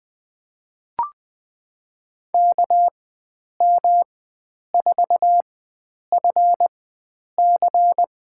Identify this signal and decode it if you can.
Morse code, heard as a keyed tone, about 20 wpm, spaced as for 9 wpm, 700 Hz. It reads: KM4FC
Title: USA Call Signs - 4x Character Spacing - 20wpm